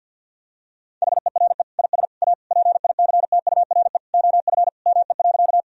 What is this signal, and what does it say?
Morse code, heard as a keyed tone, about 53 wpm, 700 Hz. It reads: HERE IS A PICTURE OF ME.
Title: Sentences from Top 1000 Words - Review Part 1 - 50wpm